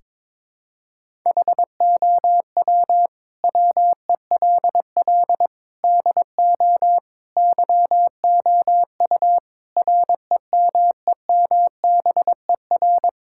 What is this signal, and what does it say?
Morse code, heard as a keyed tone, about 22 wpm, 700 Hz. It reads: HOW WELL DO YOU REMEMBER
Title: Sentences from Top 1000 Words - Review Part 5 - 22wpm